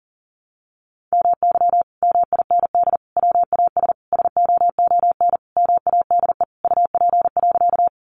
Morse code, heard as a keyed tone, about 40 wpm, 700 Hz. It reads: MY MIND WAS SOON MADE UP.
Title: Sentences from Top 500 Words 40wpm